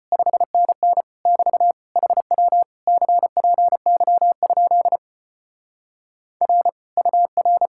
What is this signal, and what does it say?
Morse code, heard as a keyed tone, about 33 wpm, 700 Hz. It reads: 5NN <BT> HW CPY? R UR